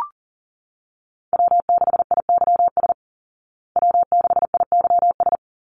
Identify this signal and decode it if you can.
Morse code, heard as a keyed tone, about 40 wpm, 700 Hz. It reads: W6IYS W6IYS
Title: Call Signs Repeated 2x - 40wpm